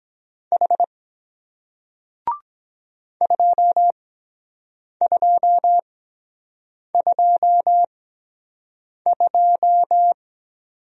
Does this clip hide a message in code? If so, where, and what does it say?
Yes: Morse code from 0.5 to 10.1 s, heard as a keyed tone, about 23 wpm, spaced as for 7 wpm, 700 Hz. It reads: H 2222